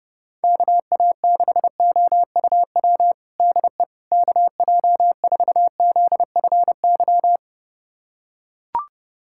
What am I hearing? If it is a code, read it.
Morse code, heard as a keyed tone, about 32 wpm, 700 Hz. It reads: KA6OUW DE KJ4ZFY
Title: USA Callsign DE Callsign - 6 Characters - 30wpm